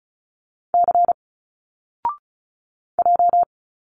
Morse code, heard as a keyed tone, about 36 wpm, 700 Hz. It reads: C J
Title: Single Letters 35wpm